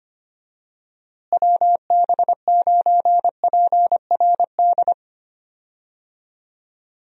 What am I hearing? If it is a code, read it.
Morse code, heard as a keyed tone, about 25 wpm, 700 Hz. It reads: WB9PRD
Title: USA Call Signs - 6 Characters - Repeated 2x - 25wpm